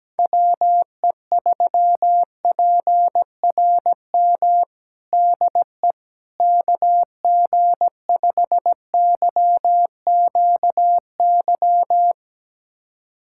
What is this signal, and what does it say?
Morse code, heard as a keyed tone, about 17 wpm, 700 Hz. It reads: WE3PRM DE KG5YQY
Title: USA Callsign DE Callsign - 6 Characters - 17wpm